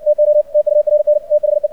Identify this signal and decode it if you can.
Morse code, heard as a keyed tone, about 56 wpm, 600 Hz, with radio noise. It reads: TO TURN THE
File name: archived / CRSnet